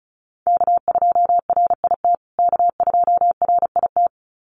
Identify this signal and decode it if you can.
Morse code, heard as a keyed tone, about 34 wpm, 700 Hz. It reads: K2RIT K2RIT
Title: USA Call Signs Repeated 2x 35wpm